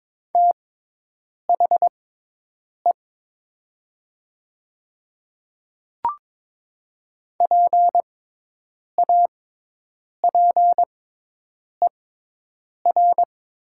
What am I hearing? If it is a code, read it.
Morse code, heard as a keyed tone, about 21 wpm, spaced as for 8 wpm, 700 Hz. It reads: THE PAPER
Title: 6x Character Spacing - Top 300 Words - 22wpm